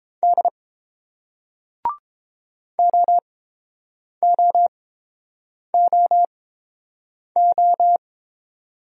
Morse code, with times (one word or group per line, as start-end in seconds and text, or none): 0.2-0.5 D
2.8-8.0 OOOO